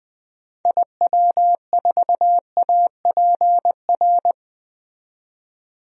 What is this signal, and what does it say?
Morse code, heard as a keyed tone, about 21 wpm, 700 Hz. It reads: IW4APR
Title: DXCC-100 Call Signs - 6 Characters - 20wpm